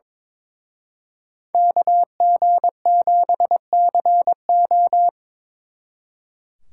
Morse code, heard as a keyed tone, about 27 wpm, 700 Hz. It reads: KG7CO
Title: Speed Racing - USA 2x2 Call Signs 22wpm